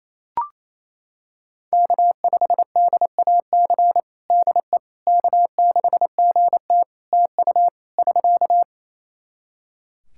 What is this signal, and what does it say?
Morse code, heard as a keyed tone, about 28 wpm, 700 Hz. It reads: K5DAC DE K6GT TU <SK>
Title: Practice QSO - Exchange 4 - Ending - 28wpm